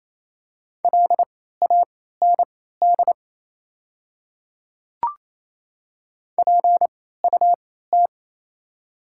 Morse code, heard as a keyed tone, about 28 wpm, spaced as for 16 wpm, 690 Hz. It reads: LAND PUT
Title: 3x Character Spacing - Top 500 Words - 28wpm